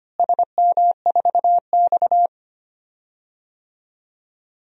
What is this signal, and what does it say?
Morse code, heard as a keyed tone, about 25 wpm, 700 Hz. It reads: SM4X
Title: DXCC-100 Call Signs - 4 Characters - Repeated 3x - 25wpm